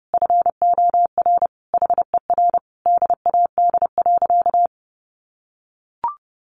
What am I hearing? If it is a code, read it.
Morse code, heard as a keyed tone, about 30 wpm, 700 Hz. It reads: FOR HER DAD.
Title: Additional Easy Phrases 30wpm